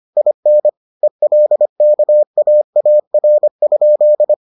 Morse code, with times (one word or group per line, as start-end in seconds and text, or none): 0.2-0.7 IN
1.0-4.3 ELKAAR?